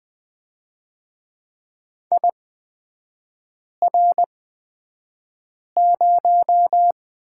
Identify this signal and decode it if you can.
Morse code, heard as a keyed tone, about 20 wpm, spaced as for 5 wpm, 700 Hz. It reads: IR0
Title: 236 Lesson 38e - Letter q - Three Characters 20wpm